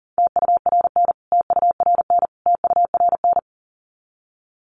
Morse code, heard as a keyed tone, about 40 wpm, 700 Hz. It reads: TURN TURN TURN
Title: Top 200 Words Repeated 40wpm